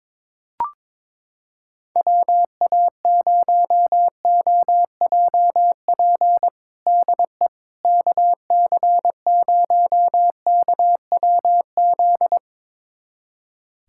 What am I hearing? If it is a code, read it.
Morse code, heard as a keyed tone, about 22 wpm, 700 Hz. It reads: WA0OJP DE KC0KWZ
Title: USA Callsign DE Callsign - 6 Characters - 22wpm